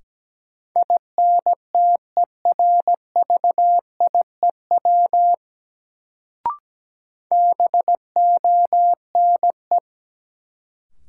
Morse code, heard as a keyed tone, about 17 wpm, 700 Hz. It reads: INTERVIEW BONE
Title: Sentences from Top 1600 Words - New Word Drill Alt - 17wpm